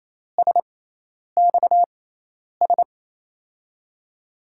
Morse code, heard as a keyed tone, about 28 wpm, spaced as for 10 wpm, 700 Hz. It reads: SXS
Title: USA Call Signs - 6 Characters - 6x Character Spacing - 28wpm